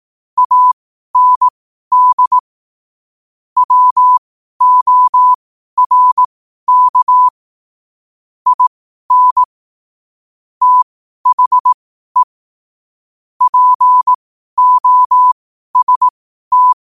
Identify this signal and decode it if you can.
Morse code, heard as a keyed tone, about 18 wpm, spaced as for 12 wpm, 1000 Hz. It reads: AND WORK IN THE POST
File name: qotdfiles / 12wpm